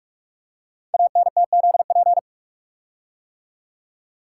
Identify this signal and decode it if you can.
Morse code, heard as a keyed tone, about 44 wpm, 690 Hz. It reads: ANTZP
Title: Code Groups - 5 Letters - 45wpm